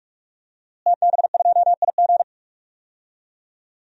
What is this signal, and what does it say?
Morse code, heard as a keyed tone, about 46 wpm, 700 Hz. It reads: TB2IG
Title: DXCC-100 Call Signs - 5 Characters - Repeated 3x - 45wpm